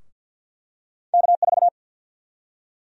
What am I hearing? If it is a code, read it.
Morse code, heard as a keyed tone, about 48 wpm, 700 Hz. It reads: K4
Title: USA Call Sign Prefixes - Repeated 2x - 50wpm